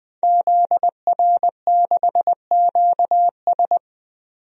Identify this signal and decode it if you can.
Morse code, heard as a keyed tone, about 20 wpm, 700 Hz. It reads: ZR6QS
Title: DXCC-100 Call Signs - 5 Characters - 20wpm